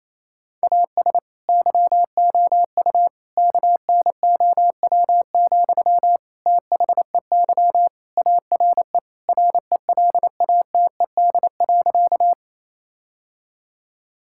Morse code, heard as a keyed tone, about 28 wpm, 700 Hz. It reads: AS YOU KNOW, THEY ARE RELATED.
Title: Sentences from Top 1000 Words - Review Part 5 - 28wpm